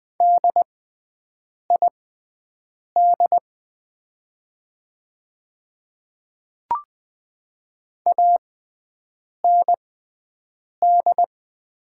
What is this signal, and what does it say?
Morse code, heard as a keyed tone, about 20 wpm, spaced as for 7 wpm, 690 Hz. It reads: DID AND
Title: 6x Character Spacing - Top 100 Words - 20wpm